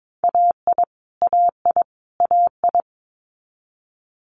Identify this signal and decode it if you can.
Morse code, heard as a keyed tone, about 22 wpm, 700 Hz. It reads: AI AI AI
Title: Multi-letter Phonemes Repeated 3x 22wpm